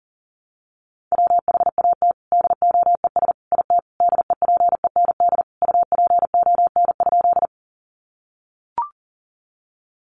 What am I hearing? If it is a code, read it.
Morse code, heard as a keyed tone, about 40 wpm, 700 Hz. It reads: WHAT DOES IT DEPEND UPON?
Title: Sentences from Top 1300 Words 40wpm